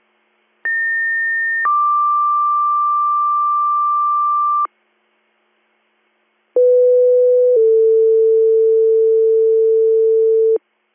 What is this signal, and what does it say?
Two-tone page, heard as a one-second tone, then a three-second tone, about 1800 then 1180 Hz, about 500 then 450 Hz. It is heard from 0.6 to 10.6 s.